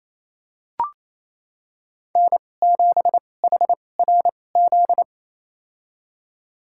Morse code, heard as a keyed tone, about 28 wpm, spaced as for 20 wpm, 700 Hz. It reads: N7HRZ
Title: USA Call Signs - 5 Characters - 2x Character Spacing - 28wpm